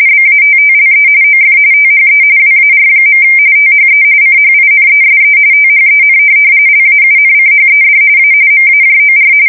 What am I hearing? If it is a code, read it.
RTTY, about 45.45 baud. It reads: BATTLE OF THE WAR. THE CHINESE THEN LAUNCHED A MAJOR
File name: fsk170 / USS-New-Jersey